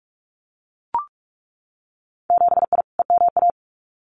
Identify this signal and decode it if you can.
Morse code, heard as a keyed tone, about 44 wpm, 700 Hz. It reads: 7I EMA
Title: Winter Field Day Exchanges Encore 45wpm